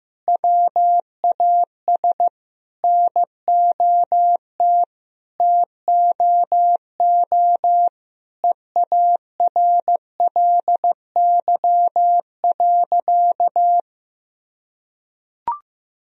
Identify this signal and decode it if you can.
Morse code, heard as a keyed tone, about 15 wpm, 700 Hz. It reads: WAS NOT TOO EARLY.
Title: Sentences from Top 1000 Words - Review Part 2 - 15wpm